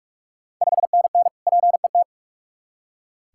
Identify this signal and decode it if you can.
Morse code, heard as a keyed tone, about 44 wpm, 690 Hz. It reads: 5NN PET
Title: Ontario QSO Party Exchange 45wpm